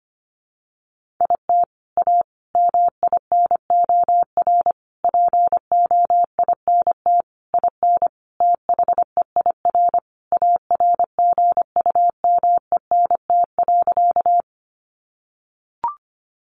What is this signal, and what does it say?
Morse code, heard as a keyed tone, about 25 wpm, 700 Hz. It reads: IT A MINOR POINT IN THEIR ARGUMENT.
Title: Sentences from Top 1500 Words 25wpm